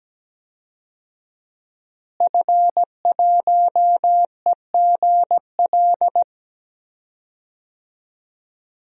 Morse code, heard as a keyed tone, about 17 wpm, 700 Hz. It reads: F1EGL
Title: DXCC-100 Call Signs - 5 Characters - Repeated 2x - 17wpm